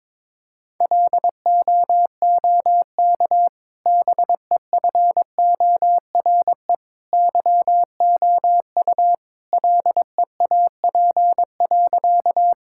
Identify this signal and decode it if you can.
Morse code, heard as a keyed tone, about 22 wpm, 700 Hz. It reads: LOOK BEFORE YOU LEAP.
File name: practice / 22wpm / Easy Sentences with Hints - 22wpm